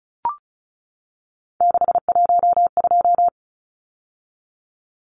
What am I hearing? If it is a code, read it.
Morse code, heard as a keyed tone, about 34 wpm, 700 Hz. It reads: 612
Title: Numbers - 3-Digit 35wpm